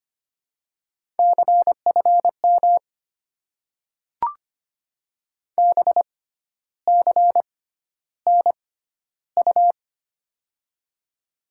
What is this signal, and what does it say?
Morse code, heard as a keyed tone, about 25 wpm, spaced as for 12 wpm, 700 Hz. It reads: CFM BCNU
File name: Top 90 QSO Elements - 6x Character Spacing - 25wpm